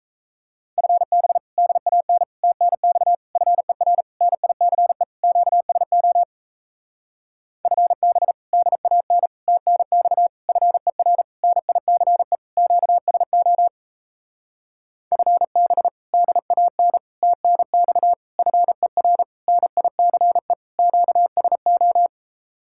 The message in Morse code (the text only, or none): FB DAN TNX FER NICE QSO FB DAN TNX FER NICE QSO FB DAN TNX FER NICE QSO